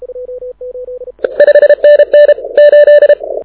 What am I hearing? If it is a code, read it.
Morse code, heard as a keyed tone, about 31 wpm, 590 Hz. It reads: 5NN 8
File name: audio zawody / IARU 2015 CW SN7Q